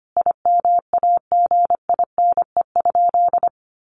Morse code, heard as a keyed tone, about 25 wpm, 700 Hz. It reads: IMAGINE?
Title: Sentences from Top 1400 Words 25wpm